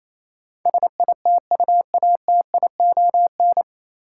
Morse code, heard as a keyed tone, about 28 wpm, 700 Hz. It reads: SITUATION
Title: Sentences from Top 1200 Words - New Word Drill Alt - 28wpm